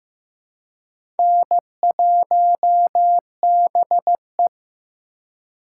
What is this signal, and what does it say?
Morse code, heard as a keyed tone, about 15 wpm, 700 Hz. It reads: N1BE
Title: USA Call Signs - 4 Characters - 4x Character Spacing - 15wpm